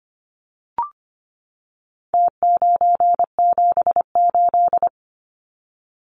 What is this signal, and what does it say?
Morse code, heard as a keyed tone, about 25 wpm, 700 Hz. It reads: T978
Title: Numbers - Contest Style - 4-Digit 25wpm